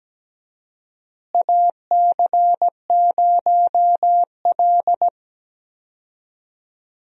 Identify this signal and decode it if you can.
Morse code, heard as a keyed tone, about 17 wpm, 700 Hz. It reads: AC0L
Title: USA Call Signs - 4 Characters - Repeated 3x - 17wpm